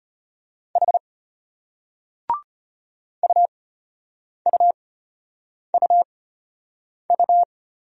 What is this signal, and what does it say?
Morse code, heard as a keyed tone, about 35 wpm, spaced as for 8 wpm, 700 Hz. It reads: H UUUU